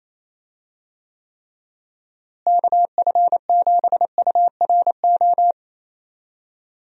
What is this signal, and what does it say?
Morse code, heard as a keyed tone, about 28 wpm, 700 Hz. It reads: KF7URO
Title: USA Call Signs - 6 Characters - Repeated 2x - 28wpm